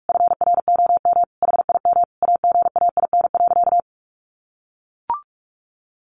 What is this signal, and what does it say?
Morse code, heard as a keyed tone, about 44 wpm, 700 Hz. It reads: FROM HIM AGAIN.
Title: Sentences from Top 1300 Words 45wpm